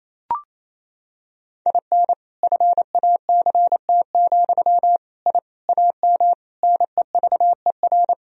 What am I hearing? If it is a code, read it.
Morse code, heard as a keyed tone, about 28 wpm, 700 Hz. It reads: IN FACT, I AM NEVER
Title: Sentences from Top 1000 Words - Review Part 3 - 28wpm